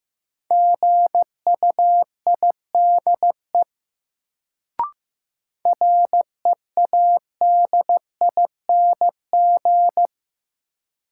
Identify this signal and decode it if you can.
Morse code, heard as a keyed tone, about 15 wpm, 700 Hz. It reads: GUIDE READING
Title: Sentences from Top 600 Words - New Word Drill - 15wpm